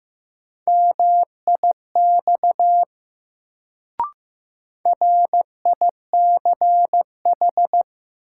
Morse code, heard as a keyed tone, about 15 wpm, 700 Hz. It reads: MIX RICH
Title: Sentences from Top 1100 Words - New Word Drill Alt - 15wpm